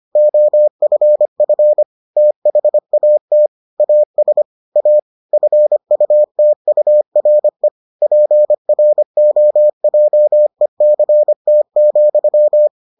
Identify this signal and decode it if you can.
Morse code, heard as a keyed tone, about 25 wpm, 600 Hz. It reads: OFF THAT AS A FUTURE PROJECT,